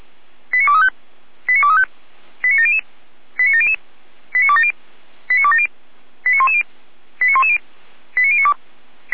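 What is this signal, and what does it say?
Five-tone selective call (ZVEI), heard as quick runs of five tones, tones of about 80 ms. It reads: 89235 89235 89700 89700 89270 89270 89110 89110 89092
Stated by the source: ZVEI five-tone selective call received via SDR, fire brigade Feuerwehr Orlamünde.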